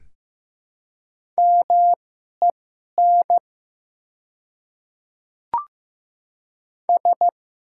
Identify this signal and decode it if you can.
Morse code, heard as a keyed tone, about 15 wpm, spaced as for 11 wpm, 700 Hz. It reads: MEN S
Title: 2x Character Spacing - Top 300 Words - 15wpm